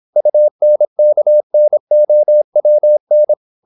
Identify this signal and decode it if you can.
Morse code, heard as a keyed tone, about 26 wpm, 600 Hz. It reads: UNKNOWN
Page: Never 26wpm | CW med Gnister